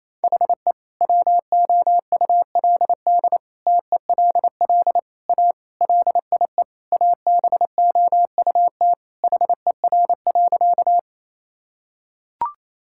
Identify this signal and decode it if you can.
Morse code, heard as a keyed tone, about 28 wpm, 700 Hz. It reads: HE WOULD TELL A LIE ABOUT HER.
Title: Sentences from Top 1600 Words 28wpm